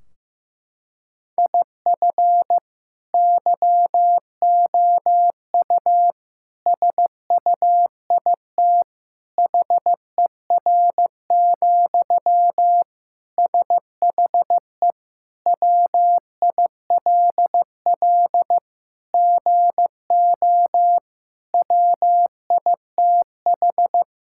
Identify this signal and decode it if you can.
Morse code, heard as a keyed tone, about 15 wpm, 700 Hz. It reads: IF YOU SUIT HER, SHE WILL GO WITH
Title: Sentences from Top 1600 Words 15wpm